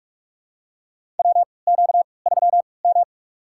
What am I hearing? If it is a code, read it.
Morse code, heard as a keyed tone, about 45 wpm, spaced as for 25 wpm, 700 Hz. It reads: WQ3M
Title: USA Call Signs - 4 Characters - 3x Character Spacing - 45wpm